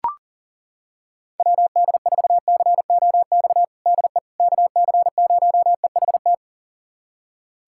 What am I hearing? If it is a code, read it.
Morse code, heard as a keyed tone, about 37 wpm, 700 Hz. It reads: WD4COX DE KC0EHT
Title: USA Callsign DE Callsign 40wpm